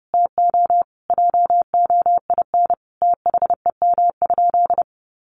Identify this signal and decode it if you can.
Morse code, heard as a keyed tone, about 30 wpm, 700 Hz. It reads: TO JOIN THEM?
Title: Sentences from Top 1400 Words 30wpm